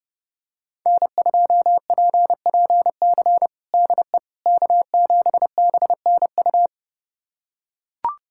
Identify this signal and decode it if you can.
Morse code, heard as a keyed tone, about 30 wpm, 700 Hz. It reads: N2PPC DE K7BNU
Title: USA Callsign DE Callsign - 5 Characters - 30wpm